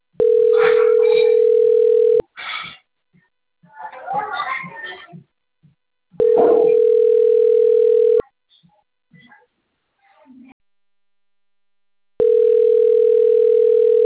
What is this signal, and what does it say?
Ringback tone (North America), 440 + 480 Hz, with speech.
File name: rg-1600-unknown- - - .49108